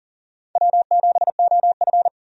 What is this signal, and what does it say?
Morse code, heard as a keyed tone, about 37 wpm, 700 Hz. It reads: W7OF
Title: USA Call Signs - 4 Characters - 40wpm